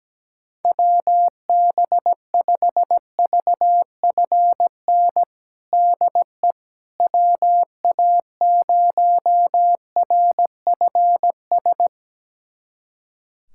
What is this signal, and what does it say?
Morse code, heard as a keyed tone, about 17 wpm, 700 Hz. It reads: WB5VFN DE WA0RFS
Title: USA Callsign DE Callsign - 6 Characters - 17wpm